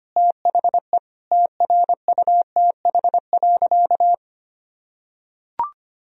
Morse code, heard as a keyed tone, about 25 wpm, 700 Hz. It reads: THE TRUTH.
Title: Sentences from Top 1000 Words - Review Part 4 - 25wpm